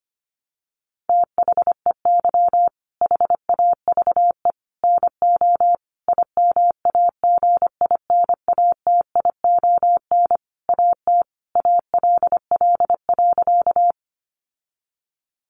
Morse code, heard as a keyed tone, about 25 wpm, 700 Hz. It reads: THEY HAVE NO IMAGINATION AT ALL.
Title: Sentences from Top 1300 Words 25wpm